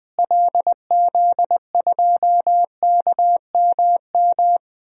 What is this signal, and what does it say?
Morse code, heard as a keyed tone, about 20 wpm, 700 Hz. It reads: LZ2KMM
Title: DXCC-100 Call Signs - 6 Characters - Repeated 2x - 20wpm